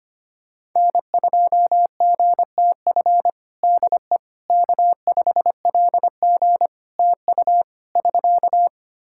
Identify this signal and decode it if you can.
Morse code, heard as a keyed tone, about 25 wpm, 700 Hz. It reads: N2GTF DE K5LG TU <SK>
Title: Practice QSO - Exchange 4 - Ending - 25wpm